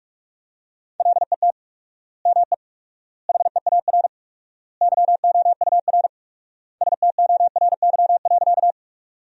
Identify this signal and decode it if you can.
Morse code, heard as a keyed tone, about 44 wpm, 700 Hz. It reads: LET ME HEAR YOUR STORY.
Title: 4x Word-Spacing - Sentences from Top 300 Words - 45wpm